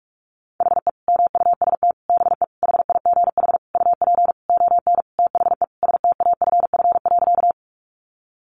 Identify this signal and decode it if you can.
Morse code, heard as a keyed tone, about 44 wpm, 700 Hz. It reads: HE MUST BE HIGH UP ON THE STAFF.
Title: Sentences from Top 1200 Words 45wpm